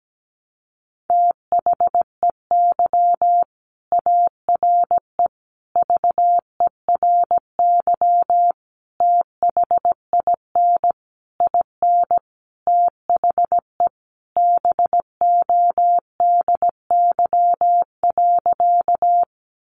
Morse code, heard as a keyed tone, about 17 wpm, 700 Hz. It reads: THEY ARE VERY THIN IN THE BODY.
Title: Sentences from Top 1200 Words 17wpm